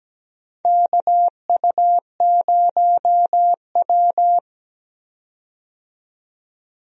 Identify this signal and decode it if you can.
Morse code, heard as a keyed tone, about 17 wpm, 700 Hz. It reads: KU0W
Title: USA Call Signs - 4 Characters - 17wpm